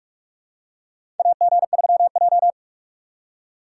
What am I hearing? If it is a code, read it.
Morse code, heard as a keyed tone, about 45 wpm, 700 Hz. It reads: AG3J
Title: USA Call Signs - 4 Characters - 3x Character Spacing - 45wpm